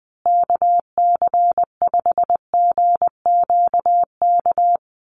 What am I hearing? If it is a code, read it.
Morse code, heard as a keyed tone, about 20 wpm, 700 Hz. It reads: KC5GQK
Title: USA Call Signs Repeated 2x 20wpm